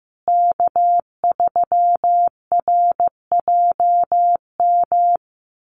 Morse code, heard as a keyed tone, about 15 wpm, 700 Hz. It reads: K3RJM
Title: USA Call Signs Repeated 2x 15wpm